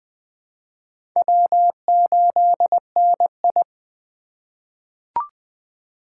Morse code, heard as a keyed tone, about 20 wpm, 700 Hz. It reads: W8NI
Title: 2023 POTA Activator Call Signs - 20wpm